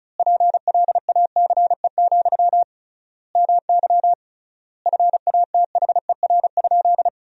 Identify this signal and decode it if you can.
Morse code, heard as a keyed tone, about 34 wpm, 700 Hz. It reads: PLACE, MY FATHER?
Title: 3x Word-Spacing - Sentences from Top 400 Words - 35wpm